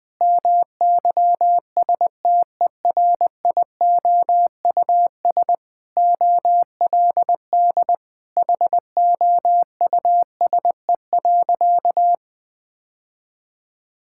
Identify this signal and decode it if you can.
Morse code, heard as a keyed tone, about 20 wpm, 700 Hz. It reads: MYSTERIOUS OLD HOUSE.
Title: Sentences with 2-Word Phrases - 20wpm